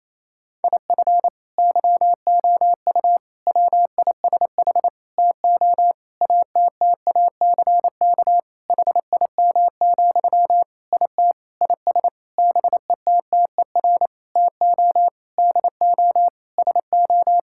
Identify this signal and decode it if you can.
Morse code, heard as a keyed tone, about 28 wpm, 700 Hz. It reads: IF YOU WISH TO ATTACK HIM, IT IS BETTER TO DO SO
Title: Sentences from Top 1600 Words 28wpm